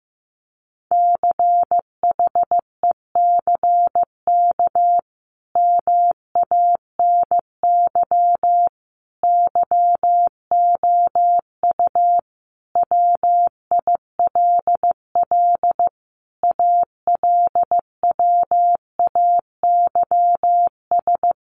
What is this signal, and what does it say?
Morse code, heard as a keyed tone, about 15 wpm, 700 Hz. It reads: CHECK MANY YOU WILL ALWAYS
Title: Sentences from Top 1500 Words 15wpm